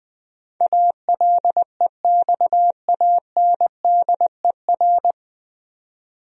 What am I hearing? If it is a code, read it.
Morse code, heard as a keyed tone, about 21 wpm, 700 Hz. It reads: ALEXANDER
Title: USA Common Names - 20wpm